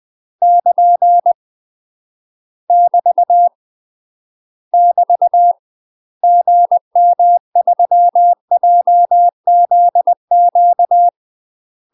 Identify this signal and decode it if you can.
Morse code, heard as a keyed tone, about 20 wpm, 700 Hz. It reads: <KN> <BT> <BT> GM3JZQ
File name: UK MASTER QSOs TRAINING/20WPM